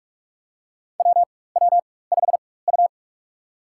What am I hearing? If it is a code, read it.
Morse code, heard as a keyed tone, about 46 wpm, spaced as for 21 wpm, 700 Hz. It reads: WW5U
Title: USA Call Signs - 4 Characters - 4x Character Spacing - 45wpm